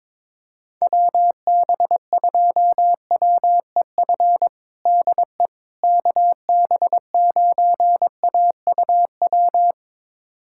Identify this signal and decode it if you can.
Morse code, heard as a keyed tone, about 22 wpm, 700 Hz. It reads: WB2WEF DE KB9AUW
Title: USA Callsign DE Callsign 22wpm